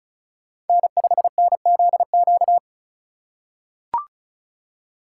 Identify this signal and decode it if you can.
Morse code, heard as a keyed tone, about 34 wpm, 700 Hz. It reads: N5NZQ
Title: USA Call Signs - 5 Characters - 4x Character Spacing - 35wpm